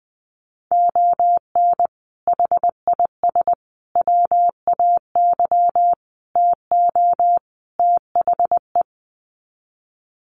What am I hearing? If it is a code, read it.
Morse code, heard as a keyed tone, about 20 wpm, 700 Hz. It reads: ON HIS WAY TO THE
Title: Sets of 5 Words Encore2 20wpm